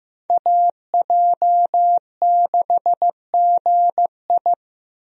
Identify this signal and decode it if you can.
Morse code, heard as a keyed tone, about 15 wpm, 700 Hz. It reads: AJ6GI
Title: USA Call Signs - 5 Characters - 15wpm